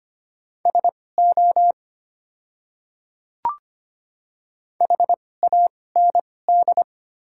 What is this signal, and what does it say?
Morse code, heard as a keyed tone, about 25 wpm, spaced as for 18 wpm, 690 Hz. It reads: SO HAND
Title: 2x Character Spacing - Top 200 Words - 25wpm